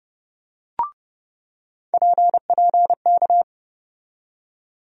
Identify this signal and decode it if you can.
Morse code, heard as a keyed tone, about 30 wpm, 700 Hz. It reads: PPK
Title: USA Call Sign Postfixes - 3 Letters 30wpm